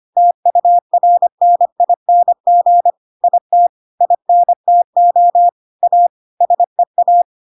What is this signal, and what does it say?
Morse code, heard as a keyed tone, about 25 wpm, 700 Hz. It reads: TURNING IT INTO A SEA
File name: Pinocchio / Pinocchio ch6 25 25 ik1hge